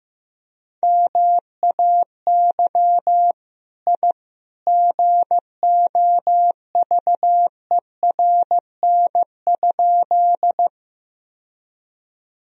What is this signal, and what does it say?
Morse code, heard as a keyed tone, about 16 wpm, 700 Hz. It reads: MAY I GOVERN?
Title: Sentences from Top 1000 Words - Review Part 2 - 15wpm